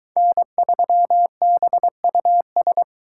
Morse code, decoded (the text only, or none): N3BUS